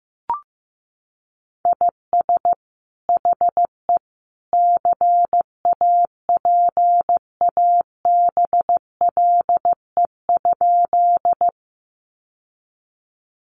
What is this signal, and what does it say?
Morse code, heard as a keyed tone, about 15 wpm, 700 Hz. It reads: IS HE CAPABLE?
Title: Sentences from Top 1200 Words 15wpm